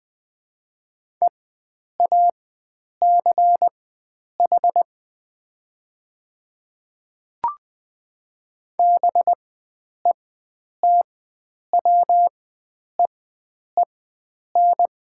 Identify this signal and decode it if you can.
Morse code, heard as a keyed tone, about 20 wpm, spaced as for 9 wpm, 700 Hz. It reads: EACH BETWEEN
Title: 4x Character Spacing - Top 300 Words - 20wpm